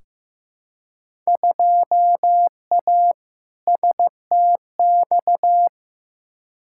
Morse code, heard as a keyed tone, about 15 wpm, 700 Hz. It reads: 2A STX
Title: Field Day Exchanges - 15wpm